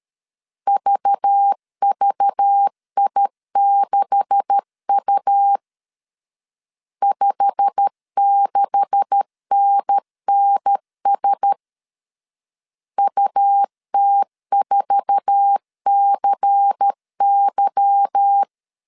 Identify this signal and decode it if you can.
Morse code, heard as a keyed tone, about 13 wpm, 800 Hz. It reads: VVI6U 56NNS UT4CY
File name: CW / DET / 1 9 1500C S C 12wpm 3811